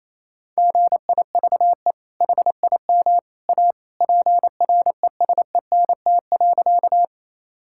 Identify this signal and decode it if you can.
Morse code, heard as a keyed tone, about 28 wpm, 700 Hz. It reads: GIVE HIM A PRESENT.
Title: Sentences from Top 1000 Words - Review Part 5 - 28wpm